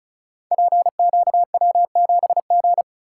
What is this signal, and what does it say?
Morse code, heard as a keyed tone, about 36 wpm, 700 Hz. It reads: PQW7G
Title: Code Groups - 5 Letter:Numbers - 35wpm